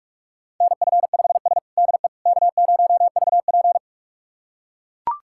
Morse code, heard as a keyed tone, about 41 wpm, 700 Hz. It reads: NF5S DE K0VP
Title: USA Callsign DE Callsign - 4 Characters - 45wpm